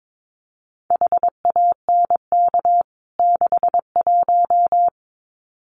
Morse code, heard as a keyed tone, about 22 wpm, 700 Hz. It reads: HANK 61
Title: CWT Exchange 22wpm